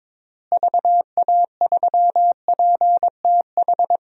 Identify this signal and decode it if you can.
Morse code, heard as a keyed tone, about 22 wpm, 700 Hz. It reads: VA3PTH
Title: DXCC-100 Call Signs - 6 Characters - Repeated 2x - 22wpm